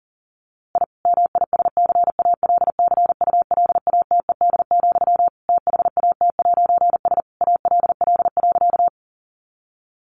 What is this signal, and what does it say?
Morse code, heard as a keyed tone, about 40 wpm, 700 Hz. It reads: I MISCALCULATED, THAT'S ALL.
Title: A Sound Of Thunder 40wpm